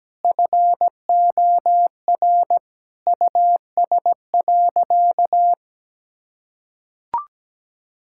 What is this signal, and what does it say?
Morse code, heard as a keyed tone, about 17 wpm, 700 Hz. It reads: FOR US.
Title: Sentences from Top 1000 Words - Review Part 5 - 17wpm